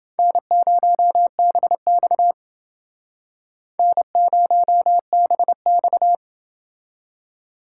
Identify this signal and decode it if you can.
Morse code, heard as a keyed tone, about 28 wpm, 690 Hz. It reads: N0BX N0BX